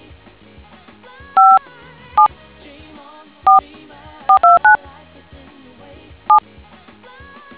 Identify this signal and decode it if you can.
DTMF, heard as telephone touch tones, tones of about 130 ms. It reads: 574429 7